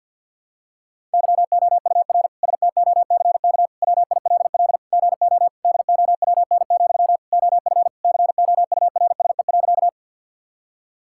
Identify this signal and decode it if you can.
Morse code, heard as a keyed tone, about 52 wpm, 690 Hz. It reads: YOUR STOCK WILL GO DOWN, OF COURSE.
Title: Sentences from Top 1000 Words - Review Part 5 - 50wpm